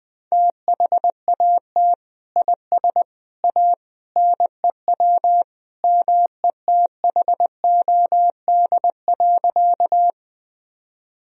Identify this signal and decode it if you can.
Morse code, heard as a keyed tone, about 20 wpm, 700 Hz. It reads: THAT IS A NEW METHOD.
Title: Sentences from Top 1000 Words - Review Part 3 - 20wpm